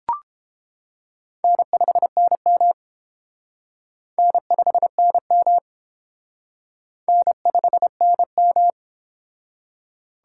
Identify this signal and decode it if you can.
Morse code, heard as a keyed tone, about 30 wpm, 700 Hz. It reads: N5NM N5NM N5NM